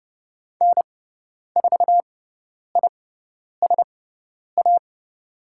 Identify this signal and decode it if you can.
Morse code, heard as a keyed tone, about 30 wpm, spaced as for 10 wpm, 700 Hz. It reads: N4ISA
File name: Lesson 17g - Letter u - Call Signs 30wpm